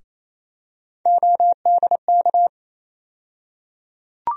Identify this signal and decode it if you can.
Morse code, heard as a keyed tone, about 28 wpm, 700 Hz. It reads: ODK
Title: USA Call Sign Postfixes - 3 Letters 28wpm